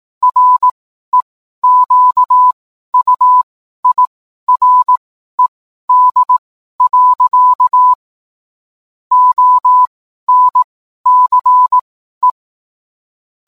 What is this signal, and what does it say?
Morse code, heard as a keyed tone, about 18 wpm, spaced as for 12 wpm, 1000 Hz. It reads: REQUIRED. ONCE
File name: cwblog / cw mp3 directory / mem morse 12